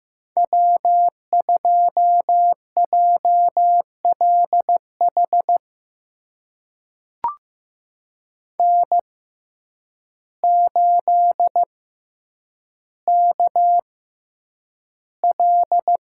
Morse code, heard as a keyed tone, about 15 wpm, spaced as for 8 wpm, 700 Hz. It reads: W2JLH N8KL
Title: USA Call Signs - 5 Characters - 6x Character Spacing - 15wpm